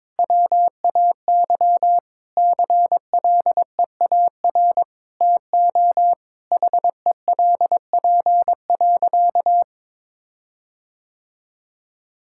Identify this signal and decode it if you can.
Morse code, heard as a keyed tone, about 22 wpm, 700 Hz. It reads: WAY CLEAR TO HELP.
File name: practice / 22wpm / Sentences with 8-Word Phrases - 22wpm